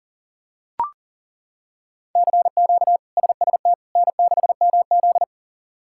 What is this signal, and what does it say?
Morse code, heard as a keyed tone, about 40 wpm, 700 Hz. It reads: CQ SST N6MZ
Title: K1USN SST QSO Practice - 40wpm